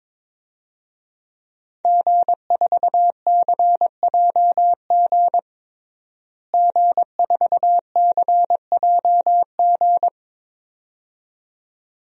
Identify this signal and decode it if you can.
Morse code, heard as a keyed tone, about 23 wpm, 700 Hz. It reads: G4CJG G4CJG
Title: DXCC-100 Call Signs - 5 Characters - Repeated 2x - 22wpm